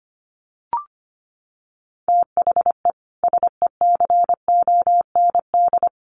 Morse code, heard as a keyed tone, about 25 wpm, 700 Hz. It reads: THE SECOND
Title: Sets of 2 Words 25wpm